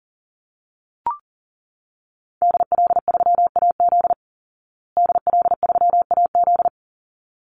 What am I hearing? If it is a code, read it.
Morse code, heard as a keyed tone, about 40 wpm, 700 Hz. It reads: DL3AZ DL3AZ
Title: Call Signs Repeated 3x - 40wpm